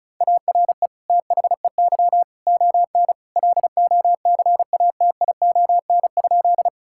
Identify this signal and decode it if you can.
Morse code, heard as a keyed tone, about 36 wpm, 690 Hz. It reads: ARE THEY ON LOCATION?
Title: Sentences from Top 1000 Words - Review Part 6 - 35wpm